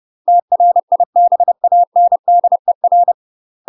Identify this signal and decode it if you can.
Morse code, heard as a keyed tone, about 30 wpm, 700 Hz. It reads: TRIBANDER
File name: UK MASTER QSOs TRAINING / 30WPM / UK MASTER QSOs 0022